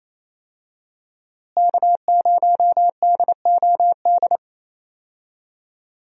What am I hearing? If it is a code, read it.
Morse code, heard as a keyed tone, about 27 wpm, 700 Hz. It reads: K0DOD
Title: USA Call Signs - 5 Characters - Repeated 3x - 28wpm